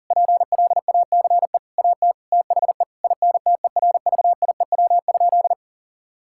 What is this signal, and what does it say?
Morse code, heard as a keyed tone, about 37 wpm, 700 Hz. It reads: PLACE AT THE INTERVIEW?
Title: Sentences from Top 1600 Words 40wpm